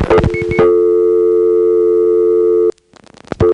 Dial tone (North America), 350 + 440 Hz.